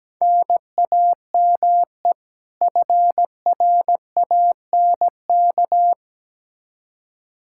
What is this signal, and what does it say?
Morse code, heard as a keyed tone, about 17 wpm, 700 Hz. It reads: NAME FRANK
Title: K1USN SST QSO Practice 17wpm